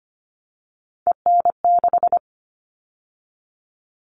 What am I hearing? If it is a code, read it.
Morse code, heard as a keyed tone, about 25 wpm, 700 Hz. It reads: EN6
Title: All Cut Numbers - 3-Digit 25wpm